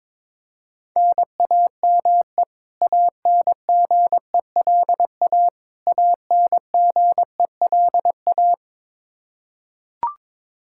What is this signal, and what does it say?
Morse code, heard as a keyed tone, about 22 wpm, 700 Hz. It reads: NAME ANGELA ANGELA
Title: Practice QSO - Exchange 1 - Beginning - 22wpm